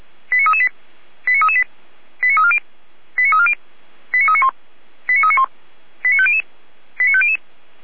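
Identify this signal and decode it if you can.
Five-tone selective call (ZVEI), heard as quick runs of five tones, tones of about 70 ms. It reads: 89338 89338 89340 89340 89391 89391 89500 89500
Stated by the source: ZVEI five-tone call 89320 received by SDR.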